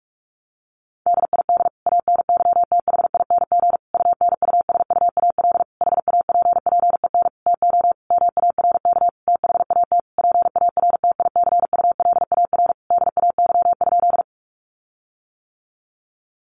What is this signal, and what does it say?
Morse code, heard as a keyed tone, about 46 wpm, 700 Hz. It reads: DID ANYTHING UNUSUAL HAPPEN TO MARK THAT PARTICULAR DAY?
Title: Sentences from Top 1400 Words 45wpm